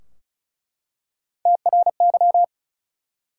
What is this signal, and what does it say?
Morse code, heard as a keyed tone, about 36 wpm, 700 Hz. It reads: TRY
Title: Three Letter Words - 35wpm